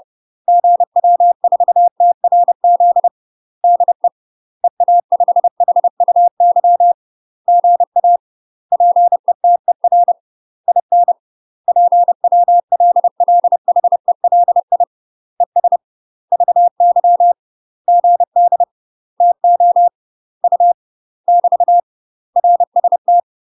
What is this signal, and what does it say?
Morse code, heard as a keyed tone, about 30 wpm, 700 Hz. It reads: GW4TRZ DE EA5HUY GA PETER IN PWLLHELI ES VY GD TO U <BT> RST
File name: UK MASTER QSOs TRAINING / 30WPM / UK MASTER QSOs 0004